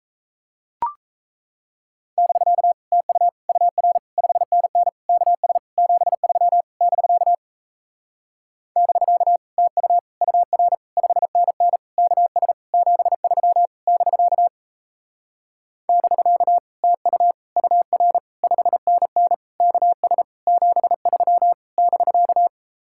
Morse code, heard as a keyed tone, about 37 wpm, 700 Hz. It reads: <BK> TU UR 5NN KS 73 <BK> <BK> TU UR 5NN KS 73 <BK> <BK> TU UR 5NN KS 73 <BK>